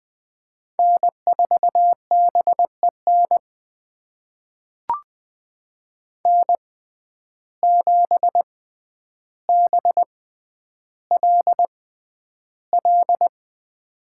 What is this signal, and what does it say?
Morse code, heard as a keyed tone, about 21 wpm, spaced as for 10 wpm, 700 Hz. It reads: N4BEN N7BLL
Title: USA Call Signs - 5 Characters - 6x Character Spacing - 20wpm